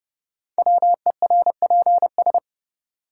Morse code, heard as a keyed tone, about 30 wpm, 700 Hz. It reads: WERPS
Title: Code Groups - 5 Letters - 30wpm